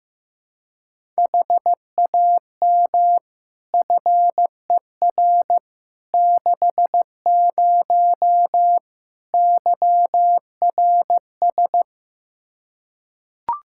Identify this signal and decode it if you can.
Morse code, heard as a keyed tone, about 15 wpm, 700 Hz. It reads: HAM FER 60 YRS
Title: Practice QSO - Exchange 3 - More Info - 15wpm